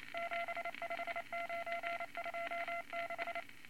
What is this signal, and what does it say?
Morse code, heard as a keyed tone, about 28 wpm, 680 Hz, with noise. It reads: 75926
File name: Morse Code Records / 22 A LCM 172 Twenty Words Per Minute